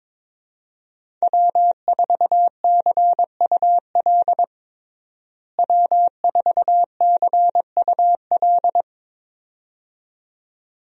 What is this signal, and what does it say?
Morse code, heard as a keyed tone, about 22 wpm, 700 Hz. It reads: W4CUL W4CUL
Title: USA Call Signs - 5 Characters - Repeated 2x - 22wpm